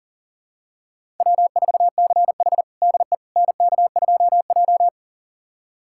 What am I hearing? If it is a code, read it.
Morse code, heard as a keyed tone, about 37 wpm, 700 Hz. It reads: W4CH DE NK2J
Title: USA Callsign DE Callsign - 4 Characters - 40wpm